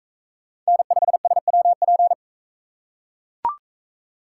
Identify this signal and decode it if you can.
Morse code, heard as a keyed tone, about 41 wpm, 700 Hz. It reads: N5SWP